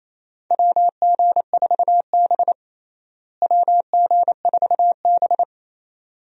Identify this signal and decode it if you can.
Morse code, heard as a keyed tone, about 28 wpm, 700 Hz. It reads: WG4B WG4B
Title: USA Call Signs - 4 Characters - Repeated 2x - 28wpm